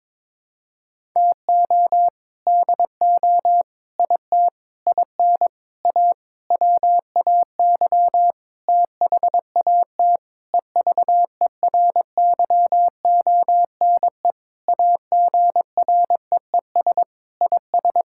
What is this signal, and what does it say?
Morse code, heard as a keyed tone, about 22 wpm, 700 Hz. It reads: TO DO IT IN A WAY THAT EVERYONE AGREES IS
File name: Sentences with 7-Word Phrases - 22wpm